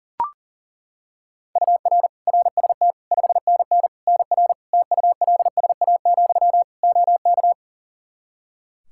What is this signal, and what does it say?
Morse code, heard as a keyed tone, about 40 wpm, 700 Hz. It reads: UR RST 5NN NR TULSA, OK
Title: Practice QSO - Exchange 1 - Beginning - 40wpm